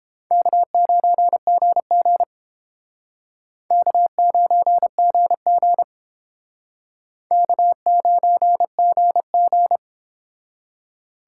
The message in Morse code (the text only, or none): K9GG K9GG K9GG